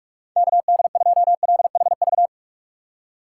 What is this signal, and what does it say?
Morse code, heard as a keyed tone, about 41 wpm, 700 Hz. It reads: KD2LHV